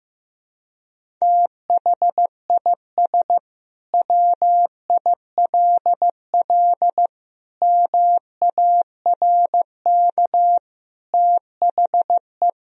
Morse code, heard as a keyed tone, about 16 wpm, 700 Hz. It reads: THIS WILL MARK THE
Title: Sentences from Top 1000 Words - Review Part 1 - 15wpm